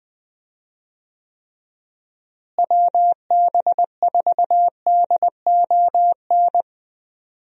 Morse code, heard as a keyed tone, about 20 wpm, 700 Hz. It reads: WB4DON
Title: USA Call Signs - 6 Characters - Repeated 2x - 20wpm